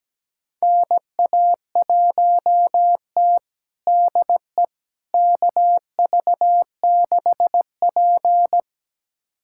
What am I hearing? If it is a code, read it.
Morse code, heard as a keyed tone, about 17 wpm, 700 Hz. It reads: NA1T DE KV6P
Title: USA Callsign DE Callsign - 4 Characters - 17wpm